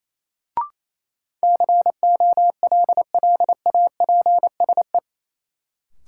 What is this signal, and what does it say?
Morse code, heard as a keyed tone, about 28 wpm, 700 Hz. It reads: COLLAPSE
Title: Sets of 5 Words Encore - New Word Drill - 28wpm